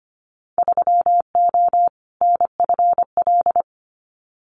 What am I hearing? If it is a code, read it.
Morse code, heard as a keyed tone, about 25 wpm, 700 Hz. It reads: 3O NFL
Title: Winter Field Day Exchanges 25wpm